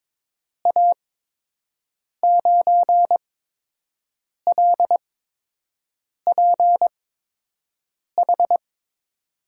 Morse code, heard as a keyed tone, about 22 wpm, spaced as for 6 wpm, 700 Hz. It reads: A9LPH
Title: USA Call Signs - 6 Characters - 8x Character Spacing - 22wpm